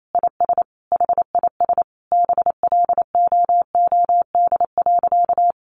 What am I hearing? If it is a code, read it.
Morse code, heard as a keyed tone, about 28 wpm, 700 Hz. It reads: IS HIS BLOOD.
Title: Sentences from Top 1100 Words 28wpm